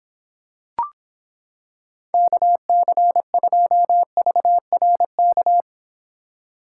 Morse code, heard as a keyed tone, about 26 wpm, 700 Hz. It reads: KC2VRK